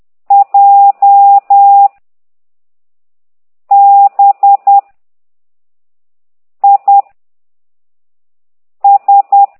Morse code, heard as a keyed tone, about 10 wpm, 800 Hz. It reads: J B I S
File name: ct2idw / exerc6